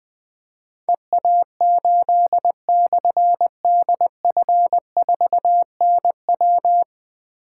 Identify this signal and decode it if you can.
Morse code, heard as a keyed tone, about 21 wpm, 700 Hz. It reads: EA8/DF4NW
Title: Difficult Call Signs 20wpm